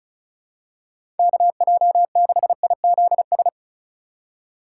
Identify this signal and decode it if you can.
Morse code, heard as a keyed tone, about 34 wpm, 700 Hz. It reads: KJ6IZS
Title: USA Call Signs - 6 Characters - 2x Character Spacing - 35wpm